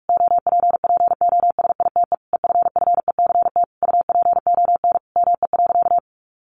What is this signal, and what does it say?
Morse code, heard as a keyed tone, about 46 wpm, 700 Hz. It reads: OPPOSITE EFFECT UPON ME.
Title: Sentences from Top 1300 Words 45wpm